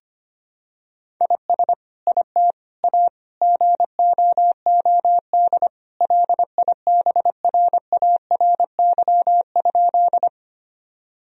Morse code, heard as a keyed tone, about 25 wpm, 700 Hz. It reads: IS IT A GOOD LIBRARY?
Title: Sentences from Top 1000 Words - Review Part 4 - 25wpm